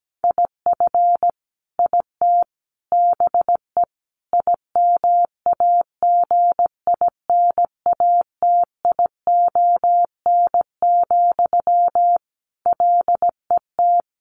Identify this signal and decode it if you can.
Morse code, heard as a keyed tone, about 17 wpm, 700 Hz. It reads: IF IT BE IMAGINATION, LET
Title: Sentences from Top 1300 Words 17wpm